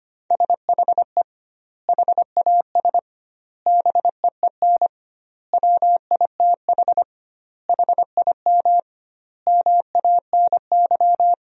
Morse code, heard as a keyed tone, about 25 wpm, 700 Hz. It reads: SHE HAS BEEN WITH HIM MANY
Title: 2x Word-Spacing - Sentences from Top 200 Words - 25wpm